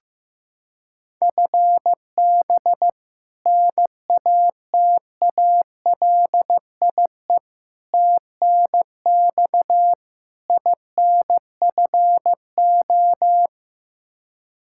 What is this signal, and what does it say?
Morse code, heard as a keyed tone, about 16 wpm, 700 Hz. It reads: FB NATALIE TNX INFO
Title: Practice QSO - Exchange 3 - More Info - 15wpm